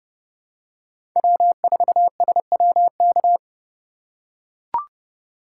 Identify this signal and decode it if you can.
Morse code, heard as a keyed tone, about 32 wpm, 700 Hz. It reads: W4SWK
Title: USA Call Signs - 5 Characters - 3x Character Spacing - 30wpm